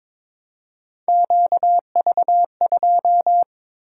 Morse code, heard as a keyed tone, about 22 wpm, 700 Hz. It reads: QV2
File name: practice / 22wpm / Three Letter-Number Combinations - 22wpm